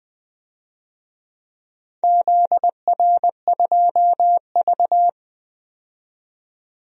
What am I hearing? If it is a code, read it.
Morse code, heard as a keyed tone, about 20 wpm, 700 Hz. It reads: ZR2V
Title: DXCC-100 Call Signs - 4 Characters - 20wpm